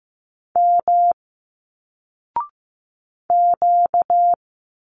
Morse code, heard as a keyed tone, about 15 wpm, 700 Hz. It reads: M Q
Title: Single Letters 15wpm